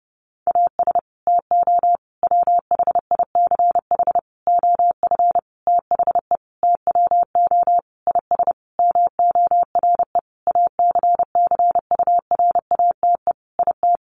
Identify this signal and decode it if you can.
Morse code, heard as a keyed tone, about 30 wpm, 700 Hz. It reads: AS TO WHICH OF THE TWO IS MORE ACCURATE IT
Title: Sentences from Top 1200 Words 30wpm